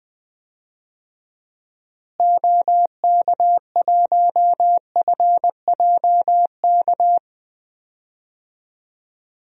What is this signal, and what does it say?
Morse code, heard as a keyed tone, about 20 wpm, 700 Hz. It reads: OK1FJK
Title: DXCC-100 Call Signs - 6 Characters - Repeated 3x - 20wpm